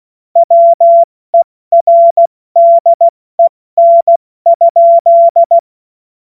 Morse code, heard as a keyed tone, about 16 wpm, spaced as for 14 wpm, 680 Hz. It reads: WERDEN?
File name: Witz-28-Oktober-2025-70-F80